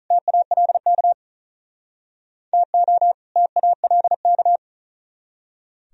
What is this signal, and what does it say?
Morse code, heard as a keyed tone, about 44 wpm, 700 Hz. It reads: TALK TO TALK